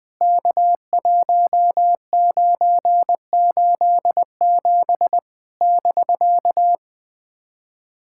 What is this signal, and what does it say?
Morse code, heard as a keyed tone, about 20 wpm, 700 Hz. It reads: K1987 <BK>
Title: POTA QSO Practice - 20wpm